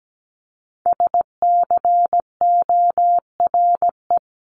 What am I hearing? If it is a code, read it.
Morse code, heard as a keyed tone, about 17 wpm, 700 Hz. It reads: SCORE
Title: Top 1501-1750 Words 17wpm